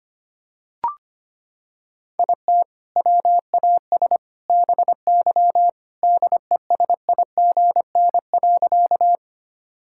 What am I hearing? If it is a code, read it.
Morse code, heard as a keyed tone, about 25 wpm, 700 Hz. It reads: IT WAS BY DESIGN.
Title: Sentences from Top 1000 Words - Review Part 5 - 25wpm